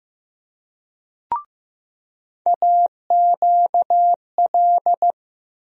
Morse code, heard as a keyed tone, about 15 wpm, 700 Hz. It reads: AQL
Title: USA Call Sign Postfixes - 3 Letters 15wpm